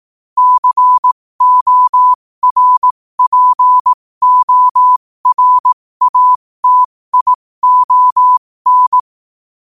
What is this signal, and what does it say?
Morse code, heard as a keyed tone, about 18 wpm, spaced as for 15 wpm, 1000 Hz. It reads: CORPORATION
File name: qotdfiles / 15wpm